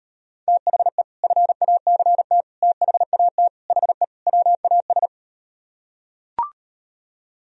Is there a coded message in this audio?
Yes: Morse code from 0.5 to 5.1 s, heard as a keyed tone, about 38 wpm, 700 Hz. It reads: THE FACT THAT HE WAS